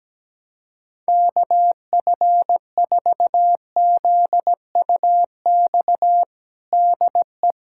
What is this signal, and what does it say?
Morse code, heard as a keyed tone, about 17 wpm, 700 Hz. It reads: KF4ZUX DE
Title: USA Callsign DE Callsign - 6 Characters - 17wpm